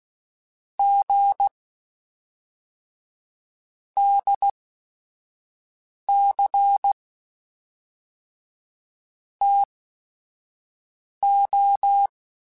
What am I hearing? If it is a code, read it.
Morse code, heard as a keyed tone, about 16 wpm, spaced as for 5 wpm, 800 Hz. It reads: G DC TO
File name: Technical Subjects / Technical Subjects-mp3s / Sw Pwr Sup @5